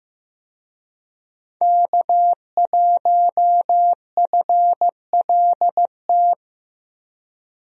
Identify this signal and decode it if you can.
Morse code, heard as a keyed tone, about 15 wpm, 700 Hz. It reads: K1FLT
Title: USA Call Signs - 5 Characters - 15wpm